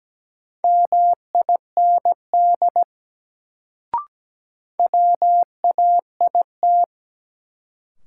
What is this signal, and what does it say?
Morse code, heard as a keyed tone, about 17 wpm, 700 Hz. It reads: MIND WAIT
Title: Sentences from Top 500 Words - New Word Drill - 17wpm